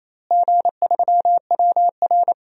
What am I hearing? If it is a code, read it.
Morse code, heard as a keyed tone, about 28 wpm, 700 Hz. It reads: G3WR
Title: DXCC-100 Call Signs - 4 Characters - Repeated 3x - 28wpm